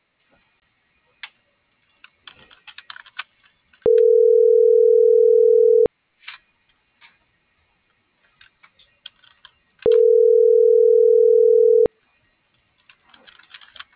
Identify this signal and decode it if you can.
Ringback tone (North America), 440 + 480 Hz.
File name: CENTRAL MABE / out miriam.soledad